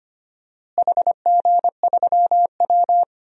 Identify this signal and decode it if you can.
Morse code, heard as a keyed tone, about 25 wpm, 700 Hz. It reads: HG3W
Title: DXCC-100 Call Signs - 4 Characters - Repeated 3x - 25wpm